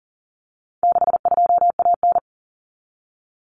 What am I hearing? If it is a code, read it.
Morse code, heard as a keyed tone, about 40 wpm, 700 Hz. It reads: 62AN
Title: Common Cut Numbers - 4-Digit 40wpm